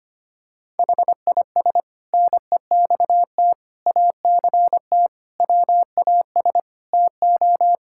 Morse code, heard as a keyed tone, about 25 wpm, 700 Hz. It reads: HIS NEXT ACT WAS TO
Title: Sentences from Top 1000 Words - Review Part 5 - 25wpm